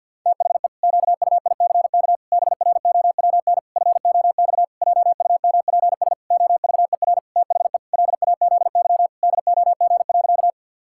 Morse code, heard as a keyed tone, about 46 wpm, 700 Hz. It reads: THE QUICK BROWN FOX JUMPS OVER THE LAZY DOG.
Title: Pangram Challenge v2 - 50wpm